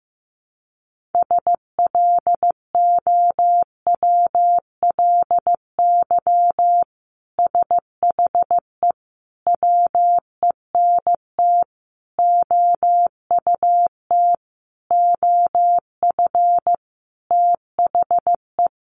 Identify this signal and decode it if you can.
Morse code, heard as a keyed tone, about 15 wpm, 700 Hz. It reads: SLOWLY SHE WENT OUT OF THE
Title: Sentences from Top 1100 Words 15wpm